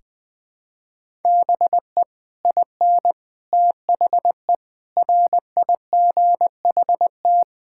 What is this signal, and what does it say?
Morse code, heard as a keyed tone, about 20 wpm, 700 Hz. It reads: BE IN THE RIGHT
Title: Sets of 9 Words - 20wpm